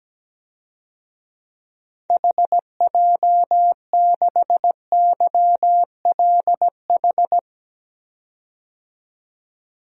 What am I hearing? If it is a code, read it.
Morse code, heard as a keyed tone, about 17 wpm, 700 Hz. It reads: HJ6YLH
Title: DXCC-100 Call Signs - 6 Characters - 17wpm